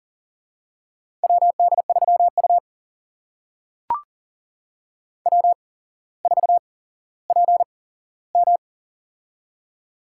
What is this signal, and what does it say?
Morse code, heard as a keyed tone, about 41 wpm, spaced as for 17 wpm, 700 Hz. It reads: WD3U W4PM
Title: USA Call Signs - 4 Characters - 8x Character Spacing - 40wpm